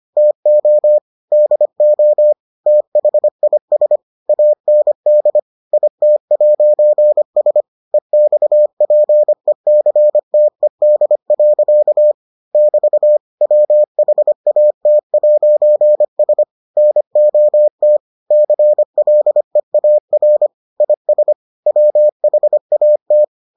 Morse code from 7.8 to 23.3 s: EXPECTED. <BT> WHAT'S NOT CLEAR IS WHAT

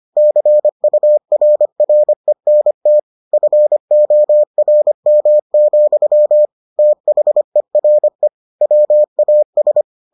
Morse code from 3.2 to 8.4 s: FORM, THERE

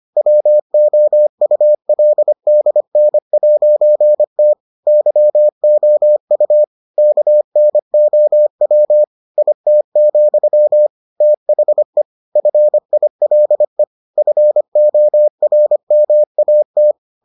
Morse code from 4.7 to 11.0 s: YOU KNOW IT,

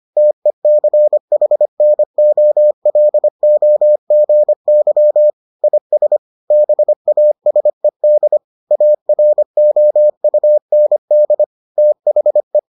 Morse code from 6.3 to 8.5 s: BASED

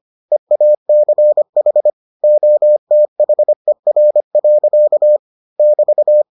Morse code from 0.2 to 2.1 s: EACH

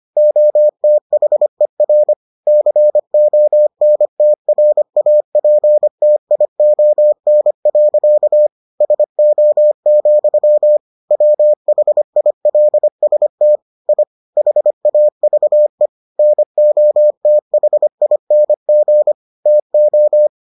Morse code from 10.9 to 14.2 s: WHILST I

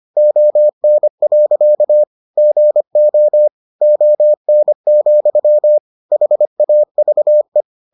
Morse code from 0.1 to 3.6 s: ON. GO

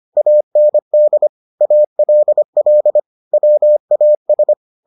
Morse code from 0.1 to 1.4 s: AND

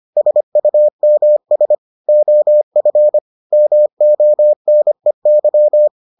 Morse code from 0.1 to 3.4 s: SUMS OF

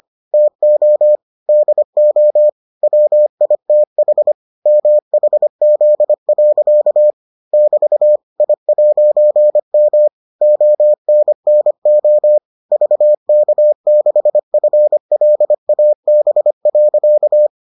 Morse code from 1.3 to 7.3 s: DO WITH MHZ.

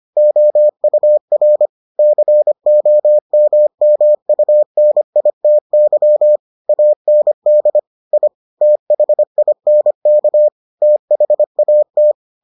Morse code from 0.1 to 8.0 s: OUR COMMUNITY AND

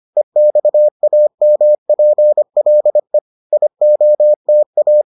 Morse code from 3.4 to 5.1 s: IOTA

